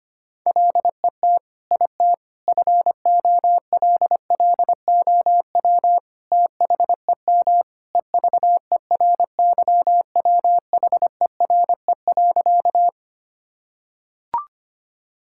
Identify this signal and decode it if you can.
Morse code, heard as a keyed tone, about 25 wpm, 700 Hz. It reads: LET IT FOLLOW THEM EVERYWHERE.
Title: Sentences from Top 1600 Words 25wpm